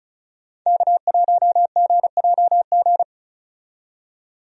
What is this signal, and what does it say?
Morse code, heard as a keyed tone, about 36 wpm, 700 Hz. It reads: K1GJG
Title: USA Call Signs - 5 Characters - 35wpm